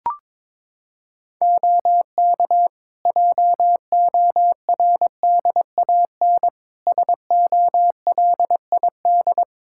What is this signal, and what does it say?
Morse code, heard as a keyed tone, about 22 wpm, 700 Hz. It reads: OK JORDAN SOLID
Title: Practice QSO - Exchange 3 - More Info - 22wpm